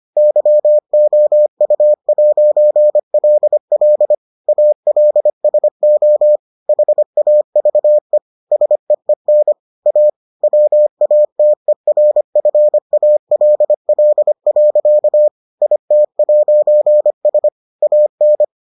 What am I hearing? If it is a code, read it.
Morse code, heard as a keyed tone, about 25 wpm, 600 Hz. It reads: YOU'LL ALSO HAVE SEEN A WATERFALL. IT'S AN